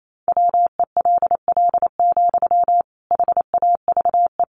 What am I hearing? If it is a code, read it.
Morse code, heard as a keyed tone, about 28 wpm, 700 Hz. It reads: WELL, HAVE
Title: Sentences from Top 800 Words 28wpm